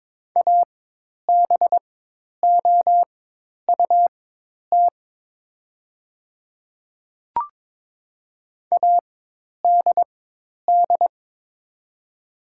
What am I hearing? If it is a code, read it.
Morse code, heard as a keyed tone, about 22 wpm, spaced as for 10 wpm, 700 Hz. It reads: ABOUT ADD
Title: 4x Character Spacing - Top 200 Words - 22wpm